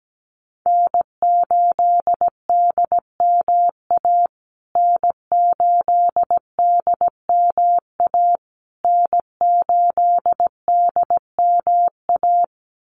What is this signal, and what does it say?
Morse code, heard as a keyed tone, about 17 wpm, 700 Hz. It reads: N8DMA N8DMA N8DMA
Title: USA Call Signs Repeated 3x 17wpm